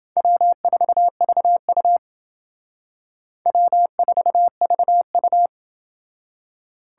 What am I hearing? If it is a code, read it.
Morse code, heard as a keyed tone, about 29 wpm, 700 Hz. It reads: W4VU W4VU